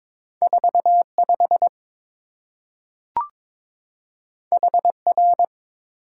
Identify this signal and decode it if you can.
Morse code, heard as a keyed tone, about 22 wpm, 700 Hz. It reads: 45 HR
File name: practice / 22wpm / Two Letter-Number Combinations - 22wpm